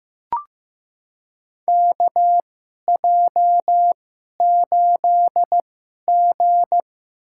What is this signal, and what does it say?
Morse code, heard as a keyed tone, about 15 wpm, spaced as for 11 wpm, 700 Hz. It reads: KJ8G
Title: USA Call Signs - 4 Characters - 2x Character Spacing - 15wpm